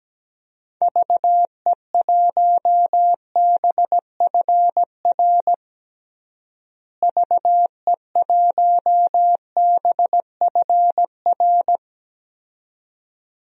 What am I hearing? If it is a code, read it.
Morse code, heard as a keyed tone, about 17 wpm, 700 Hz. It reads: VE1BFR VE1BFR
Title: DXCC-100 Call Signs - 6 Characters - Repeated 2x - 17wpm